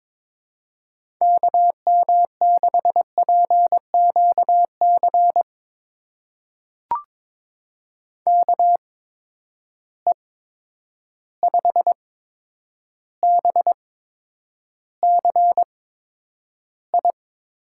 Morse code, heard as a keyed tone, about 21 wpm, spaced as for 9 wpm, 700 Hz. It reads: KM6PQC KE5BCI
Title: USA Call Signs - 6 Characters - 8x Character Spacing - 22wpm